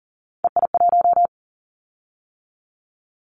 Morse code, heard as a keyed tone, about 39 wpm, 700 Hz. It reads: EI1
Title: Callsign Prefixes for Top 50 DXCC - 40wpm